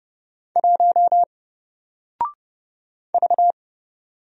Morse code, heard as a keyed tone, about 30 wpm, 700 Hz. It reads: 1 V
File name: Single-Letter-Number-30wpm